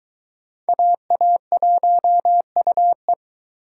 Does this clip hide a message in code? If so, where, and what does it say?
Yes: Morse code from 0.7 to 3.1 s, heard as a keyed tone, about 21 wpm, 700 Hz. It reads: AA1UE